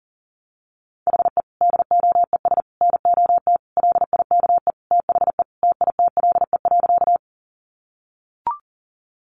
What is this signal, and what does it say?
Morse code, heard as a keyed tone, about 40 wpm, 700 Hz. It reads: HE DOES NOT LIKE THE TITLE.
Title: Sentences from Top 1200 Words 40wpm